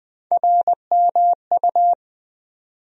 Morse code, heard as a keyed tone, about 21 wpm, 700 Hz. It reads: RMU
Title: USA Call Sign Postfixes - 3 Letters 20wpm